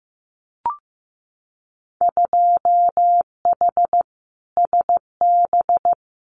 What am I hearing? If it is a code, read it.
Morse code, heard as a keyed tone, about 15 wpm, 700 Hz. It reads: 2H SB
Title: Winter Field Day Exchanges Encore 15wpm